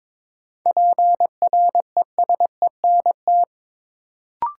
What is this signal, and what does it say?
Morse code, heard as a keyed tone, about 22 wpm, 700 Hz. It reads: PRESENT
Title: Sentences from Top 800 Words - New Word Drill - 22wpm